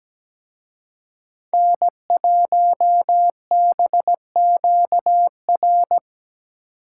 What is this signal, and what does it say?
Morse code, heard as a keyed tone, about 20 wpm, 700 Hz. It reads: N1BQR
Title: Speed Racing - USA Call Signs 17wpm